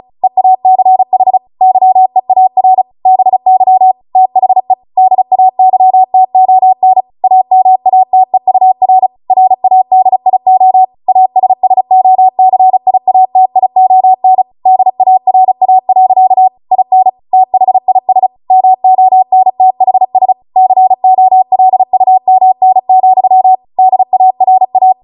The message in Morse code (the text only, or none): EACH YEAR BY THE DAYTON AMATEUR RADIO ASSOCIATION DARA. IN THIS MONTHS COLUMN, DARA